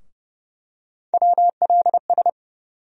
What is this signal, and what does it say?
Morse code, heard as a keyed tone, about 30 wpm, 700 Hz. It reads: WLS
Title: USA Call Sign Postfixes - 3 Letters 30wpm